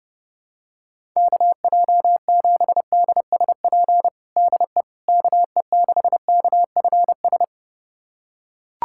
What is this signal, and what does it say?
Morse code, heard as a keyed tone, about 30 wpm, 700 Hz. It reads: KJ7DSP DE KE6KFS
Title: USA Callsign DE Callsign - 6 Characters - 30wpm